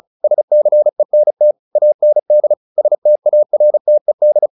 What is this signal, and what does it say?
Morse code, heard as a keyed tone, about 34 wpm, 600 Hz. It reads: SCENT AND STARTED
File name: the old man and the sea 2 35wpm